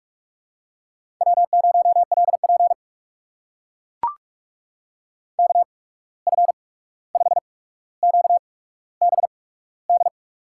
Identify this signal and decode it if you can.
Morse code, heard as a keyed tone, about 44 wpm, spaced as for 17 wpm, 700 Hz. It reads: W0LP KF5QBD
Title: USA Call Signs - 8x Character Spacing - 45wpm